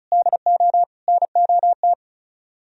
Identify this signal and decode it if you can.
Morse code, heard as a keyed tone, about 44 wpm, 700 Hz. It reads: DO NOT
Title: Speed-Racing - Sets of 4 Words 35wpm